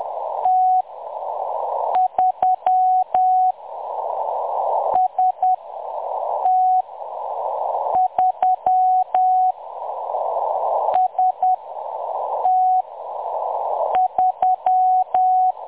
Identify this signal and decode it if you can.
Morse code, heard as a keyed tone, about 10 wpm, spaced as for 6 wpm, 740 Hz. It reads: T3 ST3 ST3
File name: M23-ST3- -0858utc-10184khz